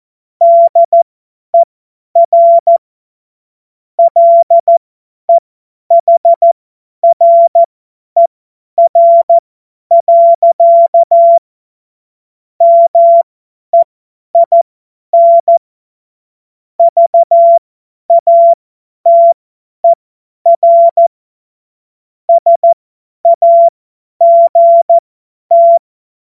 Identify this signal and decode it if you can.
Morse code, heard as a keyed tone, about 14 wpm, spaced as for 10 wpm, 680 Hz. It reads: DER LEHRER. MEIN VATER SAGT
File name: Morsekurs / Witze / Witz 07. Februar 2023 50 F70